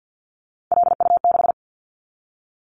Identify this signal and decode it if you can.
Morse code, heard as a keyed tone, about 50 wpm, 690 Hz. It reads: LU6
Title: Callsign Prefixes for Top 100 DXCC - 50wpm